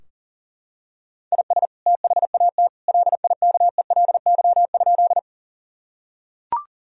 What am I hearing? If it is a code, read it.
Morse code, heard as a keyed tone, about 40 wpm, 700 Hz. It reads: IS THAT LIKELY?
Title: Sentences from Top 1000 Words - Review Part 4 - 40wpm